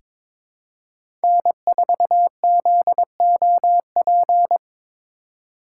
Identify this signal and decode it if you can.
Morse code, heard as a keyed tone, about 22 wpm, 700 Hz. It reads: N4ZOP
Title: USA Call Signs - 5 Characters - 2x Character Spacing - 22wpm